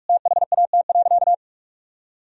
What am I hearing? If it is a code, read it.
Morse code, heard as a keyed tone, about 47 wpm, 700 Hz. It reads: THAT.
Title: Sentences from Top 1000 Words - Review Part 3 - 45wpm